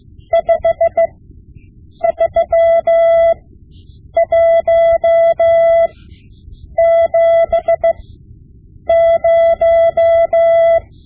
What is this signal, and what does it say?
Morse code, heard as a keyed tone, about 16 wpm, spaced as for 7 wpm, 670 Hz. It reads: 53170
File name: Morse Lessons / Morse Lessons / Tape-5B